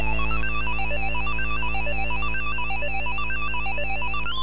There is no code.